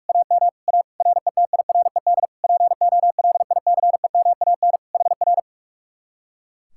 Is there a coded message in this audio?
Yes: Morse code from 0.1 to 5.4 s, heard as a keyed tone, about 58 wpm, 700 Hz. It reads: AM A RETIRED POLICEMAN HR